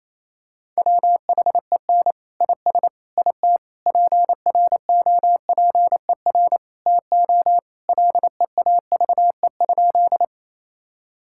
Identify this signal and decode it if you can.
Morse code, heard as a keyed tone, about 28 wpm, 700 Hz. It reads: WHEN IS IT PROPER TO LEAVE?
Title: Sentences from Top 1000 Words - Review Part 5 - 28wpm